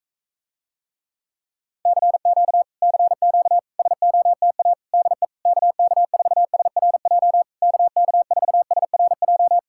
Morse code, heard as a keyed tone, about 44 wpm, 700 Hz. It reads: CQ CQ SOTA DE KK4SRJ KK4SRJ